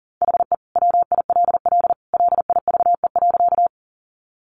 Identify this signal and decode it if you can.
Morse code, heard as a keyed tone, about 37 wpm, 700 Hz. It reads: HE WILL LIVE.
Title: Sentences from Top 200 Words 40wpm